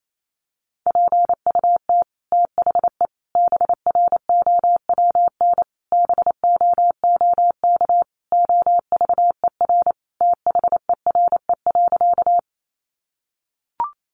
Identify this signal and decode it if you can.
Morse code, heard as a keyed tone, about 28 wpm, 700 Hz. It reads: PUT THE BROWN BOOK OVER THERE.
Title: Sentences from Top 1200 Words 28wpm